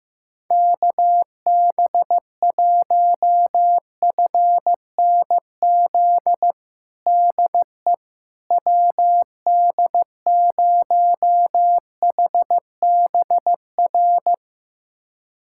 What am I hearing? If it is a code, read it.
Morse code, heard as a keyed tone, about 16 wpm, 700 Hz. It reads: KB1FNZ DE WD0HBR
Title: USA Callsign DE Callsign - 6 Characters - 15wpm